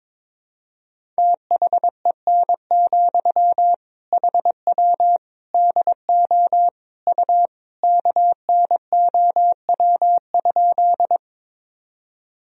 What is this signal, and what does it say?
Morse code, heard as a keyed tone, about 22 wpm, 700 Hz. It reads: THEN, HW DO U KNOW?
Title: Sentences from Top 100 Words and Abbreviations 22wpm